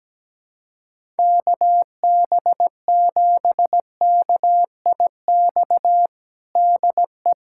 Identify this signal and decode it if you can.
Morse code, heard as a keyed tone, about 17 wpm, 700 Hz. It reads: KB7KIX DE
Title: USA Callsign DE Callsign 17wpm